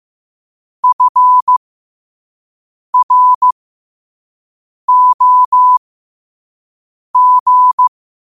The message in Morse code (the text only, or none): FROG